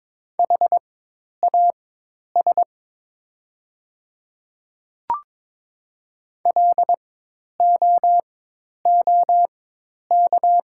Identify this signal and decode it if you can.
Morse code, heard as a keyed tone, about 22 wpm, spaced as for 10 wpm, 700 Hz. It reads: HAS LOOK
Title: 4x Character Spacing - Top 100 Words - 22wpm